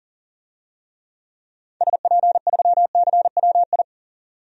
Morse code, heard as a keyed tone, about 40 wpm, 700 Hz. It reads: SP3CWI
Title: DXCC-100 Call Signs - 6 Characters - 40wpm